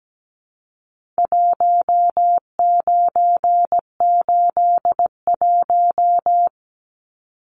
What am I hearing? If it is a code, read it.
Morse code, heard as a keyed tone, about 17 wpm, 700 Hz. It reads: 1981
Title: Numbers - Contest Style - 4-Digit 17wpm